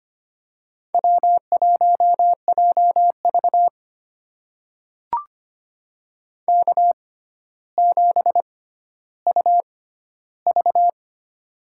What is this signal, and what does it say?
Morse code, heard as a keyed tone, about 25 wpm, spaced as for 13 wpm, 700 Hz. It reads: W1JV K7UV
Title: USA Call Signs - 4 Characters - 6x Character Spacing - 25wpm